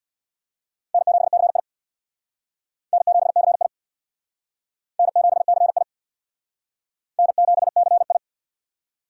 Morse code, heard as a keyed tone, about 60 wpm, 700 Hz. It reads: N7CI N7CI N7CI N7CI
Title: Speed Racing - USA 1x2 Call Signs 50wpm